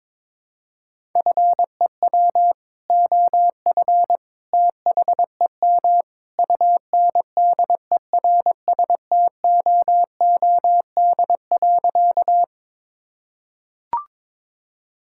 Morse code, heard as a keyed tone, about 22 wpm, 700 Hz. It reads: FEW OF THEM UNDERSTOOD.
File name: practice / 22wpm / Sentences with 2-Word Phrases - 22wpm